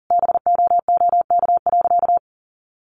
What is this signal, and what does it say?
Morse code, heard as a keyed tone, about 40 wpm, 700 Hz. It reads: BOOK.
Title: Sentences from Top 1200 Words 40wpm